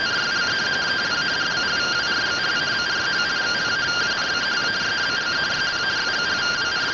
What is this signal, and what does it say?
Two-tone teleprinter signal, about 170 Hz shift.